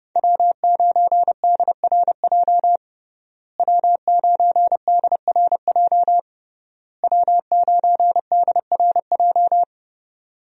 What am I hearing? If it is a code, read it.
Morse code, heard as a keyed tone, about 30 wpm, 700 Hz. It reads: W9DRJ W9DRJ W9DRJ
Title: USA Call Signs - 5 Characters - Repeated 3x - 30wpm